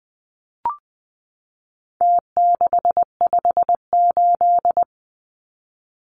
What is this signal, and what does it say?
Morse code, heard as a keyed tone, about 20 wpm, 700 Hz. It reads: T658
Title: Numbers - Contest Style - 4-Digit 20wpm